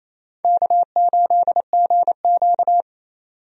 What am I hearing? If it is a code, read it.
Morse code, heard as a keyed tone, about 27 wpm, 700 Hz. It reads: K8GQ
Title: USA Call Signs - 4 Characters - 28wpm